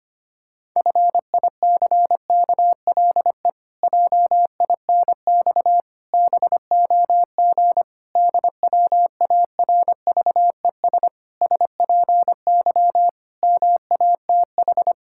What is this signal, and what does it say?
Morse code, heard as a keyed tone, about 25 wpm, 700 Hz. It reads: FICKLE JINX BOG DWARVES SPY MATH
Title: Pangram Challenge v2 - 25wpm